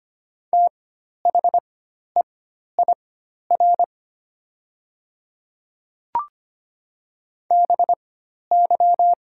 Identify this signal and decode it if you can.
Morse code, heard as a keyed tone, about 25 wpm, spaced as for 12 wpm, 700 Hz. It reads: THEIR BY
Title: 4x Character Spacing - Top 200 Words - 25wpm